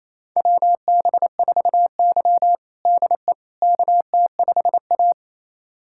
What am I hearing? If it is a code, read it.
Morse code, heard as a keyed tone, about 28 wpm, 700 Hz. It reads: WB4Y DE KT5A
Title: USA Callsign DE Callsign - 4 Characters - 28wpm